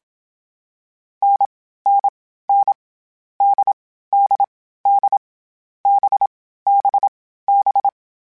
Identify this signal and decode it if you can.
Morse code, heard as a keyed tone, about 27 wpm, 800 Hz. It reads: N N N D D D B B B
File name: Variable Speed / Variable Speed-mp3s / GZ GZ7 sp @25